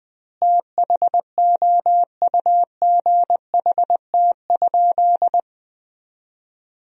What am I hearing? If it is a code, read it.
Morse code, heard as a keyed tone, about 20 wpm, 700 Hz. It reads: THOUGHT?
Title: 2x Word-Spacing - Sentences from Top 300 Words - 20wpm